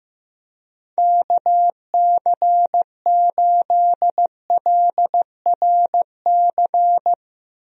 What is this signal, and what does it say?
Morse code, heard as a keyed tone, about 15 wpm, 700 Hz. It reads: KC8LRC
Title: USA Call Signs - 6 Characters - Repeated 3x - 15wpm